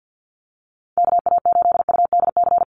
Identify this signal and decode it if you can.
Morse code, heard as a keyed tone, about 50 wpm, 700 Hz. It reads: KA8UDC
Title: USA Call Signs Repeated 2x 50wpm